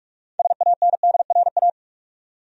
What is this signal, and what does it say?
Morse code, heard as a keyed tone, about 46 wpm, 700 Hz. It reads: SANDRA
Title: USA Common Names - 45wpm